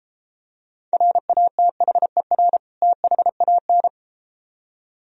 Morse code, heard as a keyed tone, about 33 wpm, 700 Hz. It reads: RATHER THAN